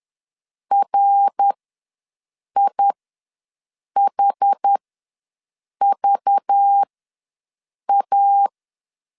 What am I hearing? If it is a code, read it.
Morse code, heard as a keyed tone, about 10 wpm, spaced as for 6 wpm, 800 Hz. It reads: RIHVA